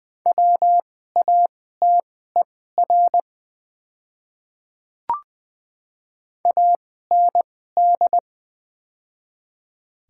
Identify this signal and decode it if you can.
Morse code, heard as a keyed tone, about 21 wpm, spaced as for 15 wpm, 700 Hz. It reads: WATER AND
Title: 2x Character Spacing - Top 200 Words - 20wpm